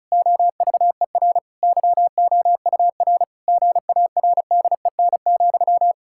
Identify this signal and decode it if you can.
Morse code, heard as a keyed tone, about 34 wpm, 700 Hz. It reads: OVER YOUR GARDEN,
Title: Pangram Challenge v2 - 35wpm